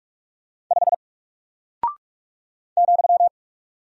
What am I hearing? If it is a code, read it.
Morse code, heard as a keyed tone, about 44 wpm, 700 Hz. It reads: 5 ,
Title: Single Character (Letter:Number:Punctuation:Prosign) Alt 45wpm